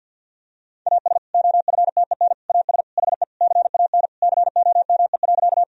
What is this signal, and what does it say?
Morse code, heard as a keyed tone, about 50 wpm, 700 Hz. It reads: AS OFTEN AS HE CAN COME.
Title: Sentences from Top 1000 Words - Review Part 1 - 50wpm